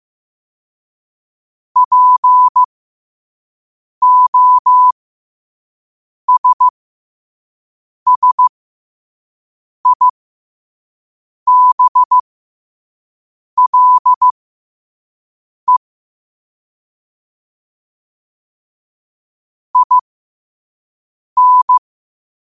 Morse code, heard as a keyed tone, about 15 wpm, spaced as for 5 wpm, 1000 Hz. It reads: POSSIBLE IN
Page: Quotes for Fri, 15 Aug 2025 in Morse Code at 5 words per minute.